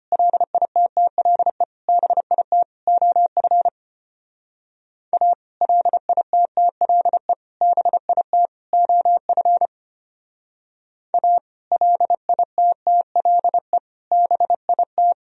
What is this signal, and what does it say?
Morse code, heard as a keyed tone, about 33 wpm, 700 Hz. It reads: LITTLE BIT OF A LITTLE BIT OF A LITTLE BIT